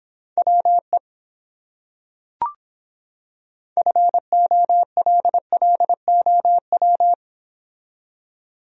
Morse code, heard as a keyed tone, about 26 wpm, 700 Hz. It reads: WE FOLLOW